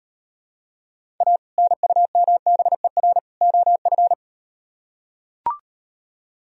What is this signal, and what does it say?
Morse code, heard as a keyed tone, about 38 wpm, 700 Hz. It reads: A NUMBER OF